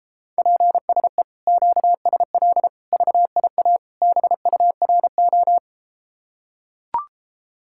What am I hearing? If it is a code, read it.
Morse code, heard as a keyed tone, about 33 wpm, 700 Hz. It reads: PSE QSL VIA BURO